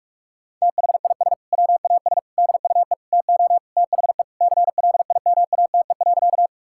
Morse code, heard as a keyed tone, about 47 wpm, 700 Hz. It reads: THIS WAS DUE TO THE CLIMATE.
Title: Sentences from Top 1600 Words 45wpm